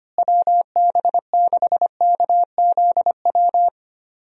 Morse code, heard as a keyed tone, about 25 wpm, 700 Hz. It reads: WB6KZW
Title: USA Call Signs - 6 Characters - Repeated 3x - 25wpm